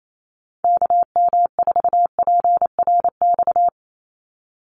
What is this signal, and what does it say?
Morse code, heard as a keyed tone, about 28 wpm, 700 Hz. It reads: KM4PRX
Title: USA Call Signs 28wpm